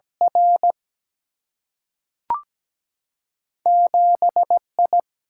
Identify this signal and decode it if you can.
Morse code, heard as a keyed tone, about 17 wpm, 700 Hz. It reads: R 7 I
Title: Single-Letter Single-Number 17wpm